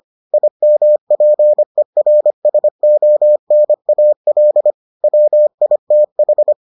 Morse code, heard as a keyed tone, about 25 wpm, 600 Hz. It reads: IMPERSONAL WITH